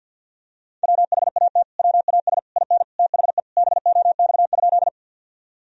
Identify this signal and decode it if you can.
Morse code, heard as a keyed tone, about 50 wpm, 700 Hz. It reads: WHAT WAS IN THE BOX?
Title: Sentences from Top 1000 Words - Review Part 2 - 50wpm